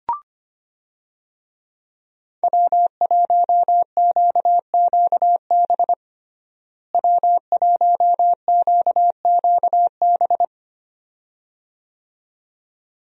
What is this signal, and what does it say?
Morse code, heard as a keyed tone, about 25 wpm, 700 Hz. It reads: W1QQB W1QQB
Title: USA Call Signs - 5 Characters - Repeated 2x - 25wpm